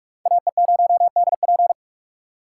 Morse code, heard as a keyed tone, about 57 wpm, 700 Hz. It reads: AE0DP